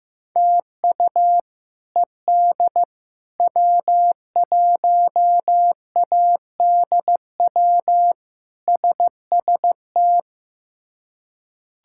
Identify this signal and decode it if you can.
Morse code, heard as a keyed tone, about 15 wpm, 700 Hz. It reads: TU ED W1ADW SST
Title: K1USN SST QSO Practice - 15wpm